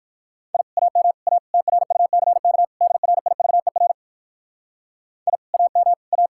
Morse code, heard as a keyed tone, about 59 wpm, 700 Hz. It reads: I AM A TRUCK DRIVER I AM A